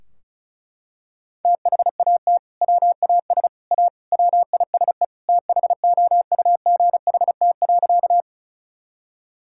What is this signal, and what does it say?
Morse code, heard as a keyed tone, about 36 wpm, 700 Hz. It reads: THAT WAS A WISE THOUGHT.
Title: Sentences from Top 1600 Words 35wpm